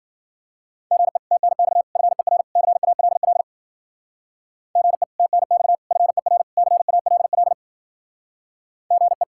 Morse code, heard as a keyed tone, about 48 wpm, 700 Hz. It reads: GE TNX FER CALL GE TNX FER CALL GE
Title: Speed-Racing Common QSO Phrases - 40wpm